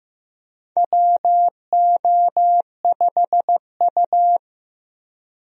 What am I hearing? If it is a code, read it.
Morse code, heard as a keyed tone, about 15 wpm, 700 Hz. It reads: WO5U
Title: USA Call Signs - 4 Characters - 15wpm